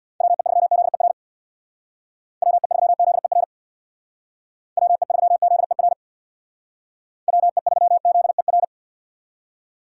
Morse code, heard as a keyed tone, about 62 wpm, 700 Hz. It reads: WE3ZER WE3ZER WE3ZER WE3ZER
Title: Speed Racing - USA 2x3 Call Signs 50wpm